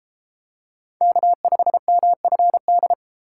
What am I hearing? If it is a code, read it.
Morse code, heard as a keyed tone, about 30 wpm, 700 Hz. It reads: K5MFD